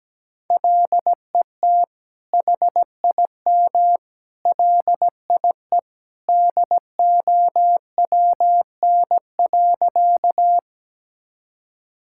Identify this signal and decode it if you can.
Morse code, heard as a keyed tone, about 17 wpm, 700 Hz. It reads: LET HIM LIE DOWN.
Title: Sentences from Top 1600 Words 17wpm